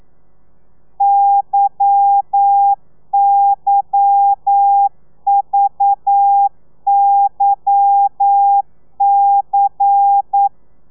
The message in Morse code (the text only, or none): YYVYC